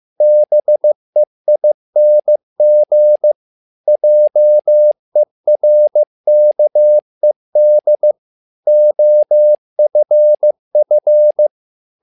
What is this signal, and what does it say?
Morse code, heard as a keyed tone, about 15 wpm, 600 Hz. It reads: BEING JERKED OFF
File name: learning center / tower trouble / tower trouble 15wpm